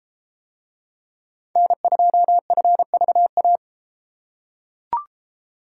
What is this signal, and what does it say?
Morse code, heard as a keyed tone, about 33 wpm, 700 Hz. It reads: N2FVA